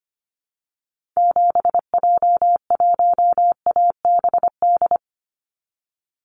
Morse code, heard as a keyed tone, about 25 wpm, 700 Hz. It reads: 7J1ABD
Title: Call Signs - 25wpm